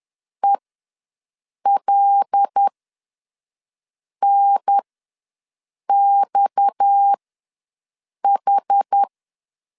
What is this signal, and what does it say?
Morse code, heard as a keyed tone, about 10 wpm, spaced as for 5 wpm, 800 Hz. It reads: ELNXH
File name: LEZIONI / 6 PKX0 / 6 PKX0 ws 10wpm 4220